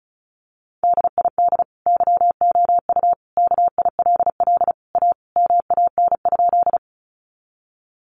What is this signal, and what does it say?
Morse code, heard as a keyed tone, about 34 wpm, 700 Hz. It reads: DID YOU KILL A MAN?
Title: Sentences from Top 1400 Words 35wpm